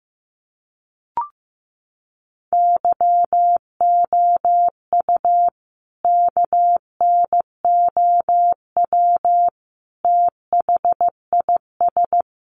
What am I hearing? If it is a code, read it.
Morse code, heard as a keyed tone, about 15 wpm, 700 Hz. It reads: YOU KNOW THIS
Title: Sentences from Top 800 Words 15wpm